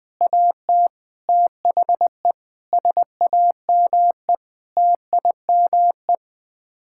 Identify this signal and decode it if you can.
Morse code, heard as a keyed tone, about 20 wpm, 700 Hz. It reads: AT THE SAME TIME
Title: Sets of 8 Words - 20wpm